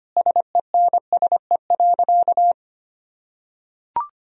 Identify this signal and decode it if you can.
Morse code, heard as a keyed tone, about 25 wpm, 700 Hz. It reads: SENSE.
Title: Sentences from Top 1000 Words - Review Part 4 - 25wpm